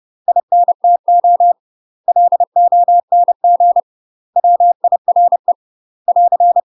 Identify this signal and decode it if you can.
Morse code, heard as a keyed tone, about 30 wpm, 700 Hz. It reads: INTO LONG WIRE <AR>
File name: UK MASTER QSOs TRAINING / 30WPM / UK MASTER QSOs 0002